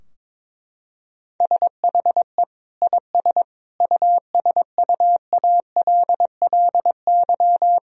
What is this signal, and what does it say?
Morse code, heard as a keyed tone, about 22 wpm, 700 Hz. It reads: SHE IS USUALLY
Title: Sentences from Top 1000 Words - Review Part 3 - 22wpm